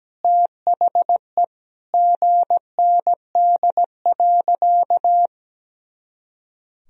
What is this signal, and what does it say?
Morse code, heard as a keyed tone, about 17 wpm, 700 Hz. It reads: THE GND.
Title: Sentences from Top 500 Words and Abbreviations 17wpm